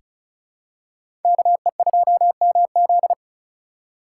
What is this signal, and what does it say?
Morse code, heard as a keyed tone, about 35 wpm, 700 Hz. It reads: KE2MZ
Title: USA Call Signs - 5 Characters - 3x Character Spacing - 35wpm